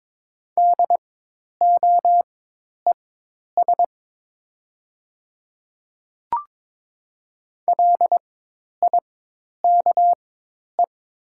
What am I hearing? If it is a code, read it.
Morse code, heard as a keyed tone, about 22 wpm, spaced as for 10 wpm, 690 Hz. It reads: DOES LIKE
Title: 4x Character Spacing - Top 500 Words - 22wpm